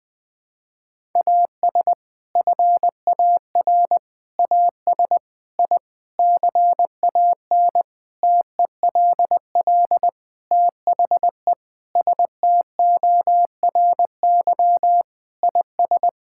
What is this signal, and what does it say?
Morse code, heard as a keyed tone, about 20 wpm, 700 Hz. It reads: AS FAR AS I CAN TELL THE STORY IS
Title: Sentences with 7-Word Phrases - 20wpm